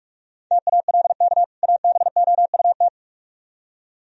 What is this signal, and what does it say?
Morse code, heard as a keyed tone, about 42 wpm, 700 Hz. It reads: TALK ABOUT